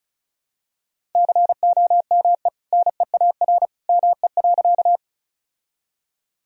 Morse code, heard as a keyed tone, about 36 wpm, 700 Hz. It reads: COME NEAR ME.
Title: Sentences from Top 1000 Words - Review Part 1 - 35wpm